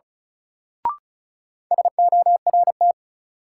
Morse code, heard as a keyed tone, about 34 wpm, 700 Hz. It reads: SORT
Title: Sentences from Top 1300 Words - New Word Drill Alt - 35wpm